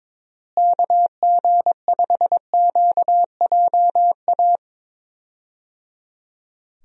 Morse code, heard as a keyed tone, about 22 wpm, 700 Hz. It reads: KG5QJA
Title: USA Call Signs - 6 Characters - Repeated 3x - 22wpm